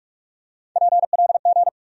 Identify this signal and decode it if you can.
Morse code, heard as a keyed tone, about 47 wpm, 700 Hz. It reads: PLG
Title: USA Call Sign Postfixes - 3 Letters 45wpm